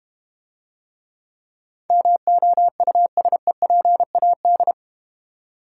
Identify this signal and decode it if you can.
Morse code, heard as a keyed tone, about 31 wpm, 700 Hz. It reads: MOUSEPAD
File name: Easy Words - Speed-Racing with Hints - 25wpm